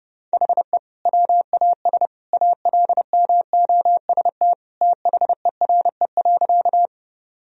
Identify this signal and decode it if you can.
Morse code, heard as a keyed tone, about 31 wpm, 700 Hz. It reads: HE WAS ALMOST THERE.
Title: Sentences from Top 1000 Words - Review Part 4 - 30wpm